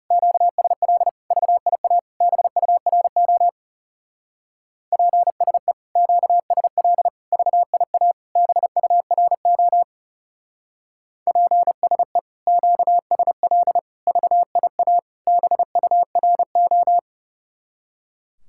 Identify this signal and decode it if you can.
Morse code, heard as a keyed tone, about 36 wpm, 700 Hz. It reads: QSL VIA BURO PSE QSL VIA BURO PSE QSL VIA BURO
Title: Speed-Racing Common QSO Phrases - 30wpm